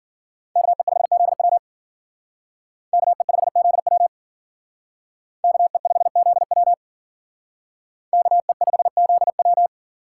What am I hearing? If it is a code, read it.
Morse code, heard as a keyed tone, about 51 wpm, 700 Hz. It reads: KE5ZW KE5ZW KE5ZW KE5ZW